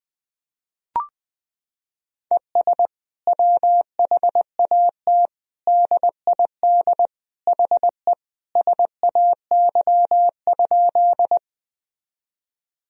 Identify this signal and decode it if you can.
Morse code, heard as a keyed tone, about 21 wpm, 700 Hz. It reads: ES WHAT DID HE SAY?
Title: Sentences from Top 200 Words and Abbreviations 20wpm